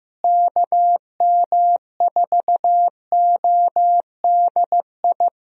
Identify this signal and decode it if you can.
Morse code, heard as a keyed tone, about 16 wpm, 700 Hz. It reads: KM4ODI
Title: USA Call Signs - 6 Characters - Repeated 2x - 15wpm